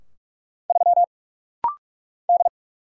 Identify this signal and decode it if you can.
Morse code, heard as a keyed tone, about 46 wpm, 700 Hz. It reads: 3 D
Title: Single Letter-Number Alt - Rapid-Fire 45wpm